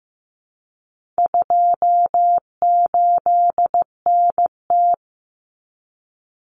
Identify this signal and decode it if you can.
Morse code, heard as a keyed tone, about 15 wpm, 700 Hz. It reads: 28NT
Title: Common Cut Numbers - 4-Digit 15wpm